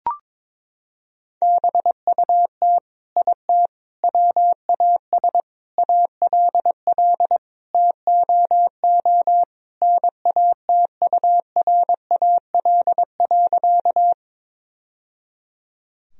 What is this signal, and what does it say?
Morse code, heard as a keyed tone, about 22 wpm, 700 Hz. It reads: BUT IT WAS ALL TOO NATURAL.
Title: Sentences from Top 1000 Words - Review Part 4 - 22wpm